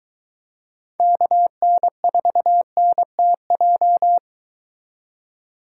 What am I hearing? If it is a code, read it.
Morse code, heard as a keyed tone, about 22 wpm, 700 Hz. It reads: KN4NTJ